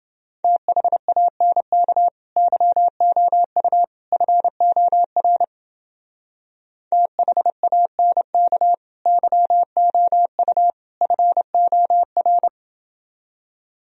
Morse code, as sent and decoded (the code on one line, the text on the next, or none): - .... .- -. -.- / -.-- --- ..- / ..-. --- .-. / - .... .- -. -.- / -.-- --- ..- / ..-. --- .-.
THANK YOU FOR THANK YOU FOR